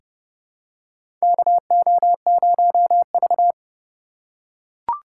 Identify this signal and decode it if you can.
Morse code, heard as a keyed tone, about 31 wpm, 700 Hz. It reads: KO0V
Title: USA Call Signs - 4 Characters - 3x Character Spacing - 30wpm